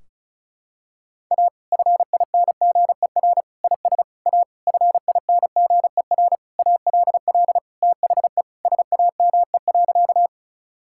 Morse code, heard as a keyed tone, about 34 wpm, 700 Hz. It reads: A FINGER IS A FINGER ALL THE SAME.
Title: Sentences from Top 1600 Words 35wpm